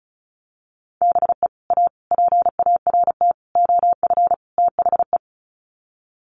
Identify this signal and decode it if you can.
Morse code, heard as a keyed tone, about 34 wpm, 700 Hz. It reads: BE A PART OF THE
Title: Sets of 5 Words Encore 35wpm